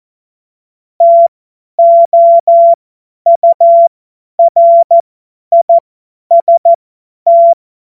Morse code, heard as a keyed tone, about 14 wpm, spaced as for 10 wpm, 680 Hz. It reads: TOURIST